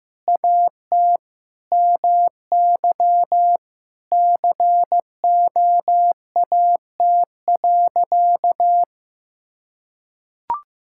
Morse code, heard as a keyed tone, about 15 wpm, 700 Hz. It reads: AT MY COAT.
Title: Sentences from Top 1600 Words 15wpm